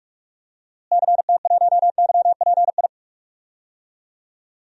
Morse code, heard as a keyed tone, about 41 wpm, 700 Hz. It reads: CT1YPI
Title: DXCC-100 Call Signs - 6 Characters - Repeated 3x - 45wpm